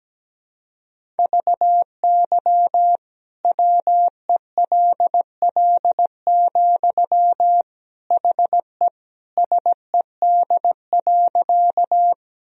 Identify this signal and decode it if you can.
Morse code, heard as a keyed tone, about 17 wpm, 700 Hz. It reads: VY WELL, HE SED.
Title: Sentences from Top 200 Words and Abbreviations 17wpm